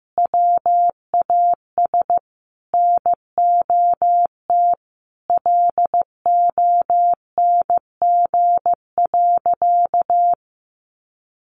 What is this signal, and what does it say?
Morse code, heard as a keyed tone, about 15 wpm, 700 Hz. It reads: WAS NOT LONG.
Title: Sentences from Top 1300 Words 15wpm